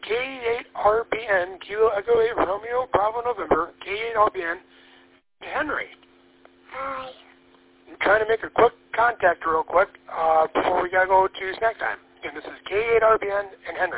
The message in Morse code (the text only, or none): none